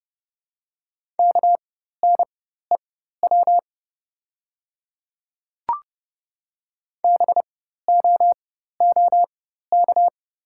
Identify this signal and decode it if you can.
Morse code, heard as a keyed tone, about 30 wpm, spaced as for 14 wpm, 700 Hz. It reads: KNEW BOOK
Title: 4x Character Spacing - Top 500 Words - 30wpm